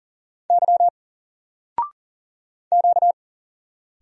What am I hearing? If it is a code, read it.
Morse code, heard as a keyed tone, about 40 wpm, 700 Hz. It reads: Y Q
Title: Single Letter-Number 40wpm